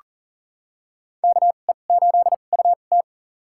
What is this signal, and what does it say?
Morse code, heard as a keyed tone, about 40 wpm, spaced as for 29 wpm, 700 Hz. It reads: KE8UT
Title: USA Call Signs - 5 Characters - 2x Character Spacing - 40wpm